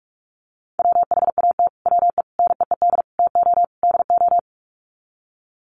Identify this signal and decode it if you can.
Morse code, heard as a keyed tone, about 44 wpm, 700 Hz. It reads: WHAT WE NEED TO DO
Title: Sets of 5 Words Encore 45wpm